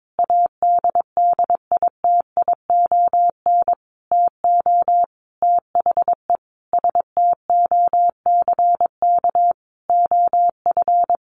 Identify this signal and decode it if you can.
Morse code, heard as a keyed tone, about 22 wpm, 700 Hz. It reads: ADDITION TO THE STOCK OF
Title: Sentences from Top 1500 Words 22wpm